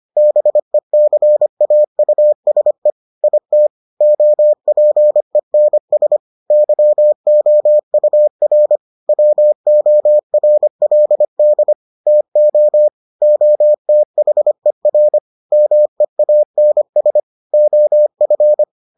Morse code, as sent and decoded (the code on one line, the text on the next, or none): -... -.-. .- ..- ... / .. - / --- .--. -. ... / -.-- --- ..- .-. / .-- --- .-. .-.. -.. / - --- / --- - .... .-. / -- . .- -. ... / --- ..-.
BECAUSE IT OPENS YOUR WORLD TO OTHER MEANS OF